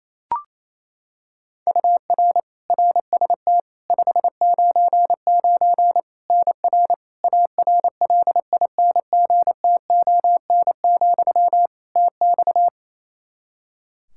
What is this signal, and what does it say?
Morse code, heard as a keyed tone, about 28 wpm, 700 Hz. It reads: UR RST 599 NR ARLINGTON, TX
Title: Practice QSO - Exchange 1 - Beginning - 28wpm